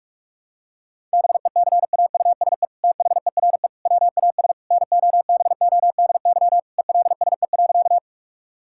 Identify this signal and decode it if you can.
Morse code, heard as a keyed tone, about 46 wpm, 700 Hz. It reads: BECAUSE THERE WAS NOBODY ELSE.
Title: Sentences from Top 1600 Words 45wpm